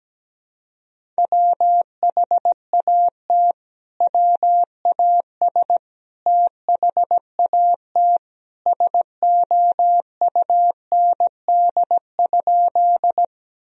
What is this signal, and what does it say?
Morse code, heard as a keyed tone, about 17 wpm, 700 Hz. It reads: WHAT WAS THAT SOUND?
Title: Sentences from Top 1000 Words - Review Part 1 - 17wpm